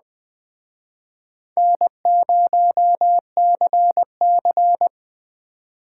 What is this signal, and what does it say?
Morse code, heard as a keyed tone, about 22 wpm, 700 Hz. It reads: N0CC